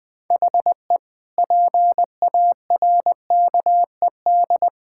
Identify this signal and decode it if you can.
Morse code, heard as a keyed tone, about 20 wpm, 700 Hz. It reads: HE PARKED
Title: Sentences with 5-Word Phrases - Spoken Beginning - 20wpm